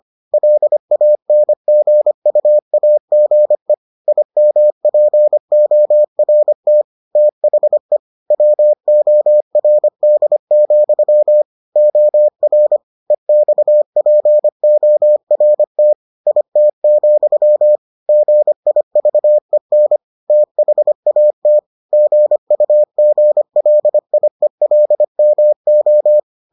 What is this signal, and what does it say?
Morse code, heard as a keyed tone, about 25 wpm, 600 Hz. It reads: LANGUAGE IMPORT THE WORD, OR EXPORT IT, GIVEN THAT GUGLIELMO